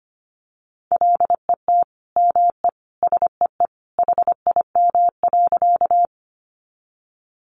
Morse code, heard as a keyed tone, about 25 wpm, 700 Hz. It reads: LET ME SEE HIM.
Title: Sentences from Top 300 Words 25wpm